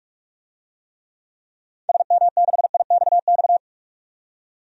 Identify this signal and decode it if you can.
Morse code, heard as a keyed tone, about 45 wpm, 700 Hz. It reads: SM6IXX
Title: DXCC-100 Call Signs - 6 Characters - 45wpm